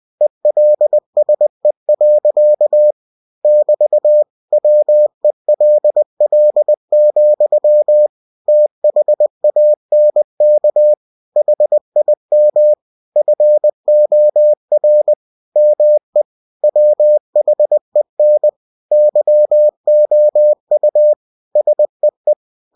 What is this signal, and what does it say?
Morse code, heard as a keyed tone, about 20 wpm, 600 Hz. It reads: ELSE. <BT> WELL, THANK HIM FOR ME WHEN YOU SEE
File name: cw-books / winnie-the-pooh / winnie-20wpm-page-0089